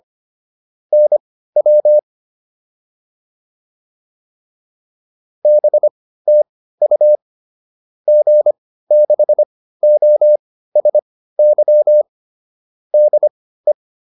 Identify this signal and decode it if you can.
Morse code, heard as a keyed tone, about 25 wpm, spaced as for 15 wpm, 600 Hz. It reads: NW BTU G6OSY DE
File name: QSO409 15